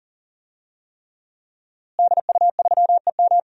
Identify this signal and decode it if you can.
Morse code, heard as a keyed tone, about 39 wpm, 700 Hz. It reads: DU3EM
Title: DXCC-100 Call Signs - 5 Characters - Repeated 3x - 40wpm